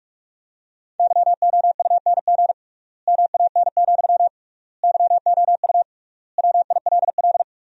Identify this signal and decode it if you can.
Morse code, heard as a keyed tone, about 46 wpm, 700 Hz. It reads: YOUNG MAN, YOU WILL
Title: 3x Word-Spacing - Sentences from Top 400 Words - 45wpm